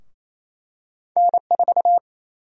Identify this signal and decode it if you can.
Morse code, heard as a keyed tone, about 28 wpm, 700 Hz. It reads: N4
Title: USA Call Sign Prefixes - 28wpm